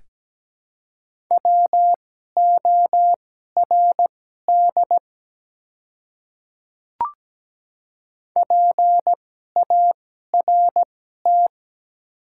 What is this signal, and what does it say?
Morse code, heard as a keyed tone, about 17 wpm, spaced as for 12 wpm, 700 Hz. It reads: WORD PART
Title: 2x Character Spacing - Top 200 Words - 17wpm